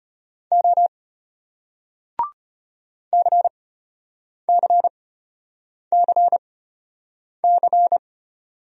Morse code, heard as a keyed tone, about 33 wpm, spaced as for 8 wpm, 700 Hz. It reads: O CCCC